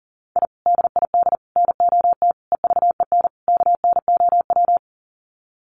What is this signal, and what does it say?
Morse code, heard as a keyed tone, about 40 wpm, 700 Hz. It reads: I DID NOT EVEN KNOW
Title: Sets of 5 Words Encore 40wpm